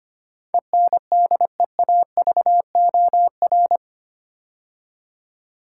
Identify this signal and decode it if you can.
Morse code, heard as a keyed tone, about 25 wpm, 700 Hz. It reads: ENDEAVOR
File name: 8 Letter Words - First 3 Letters Given - 25wpm